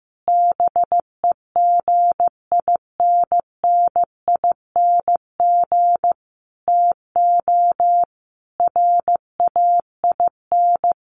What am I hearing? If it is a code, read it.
Morse code, heard as a keyed tone, about 15 wpm, 700 Hz. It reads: BEGINNING TO RAIN
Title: Sentences from Top 800 Words 15wpm